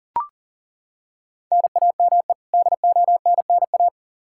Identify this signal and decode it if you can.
Morse code, heard as a keyed tone, about 40 wpm, 700 Hz. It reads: NAME DONNA
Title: Practice QSO - Exchange 1 - Beginning - 40wpm